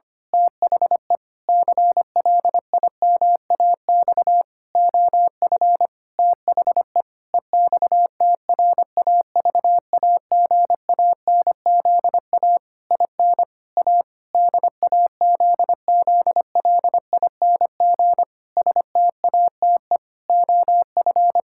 Morse code, heard as a keyed tone, about 25 wpm, 700 Hz. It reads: THE CLIMAX OF THE EXTRAVAGANZA IN A DAZZLING STATE OF
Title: Pangram Challenge v2 - 25wpm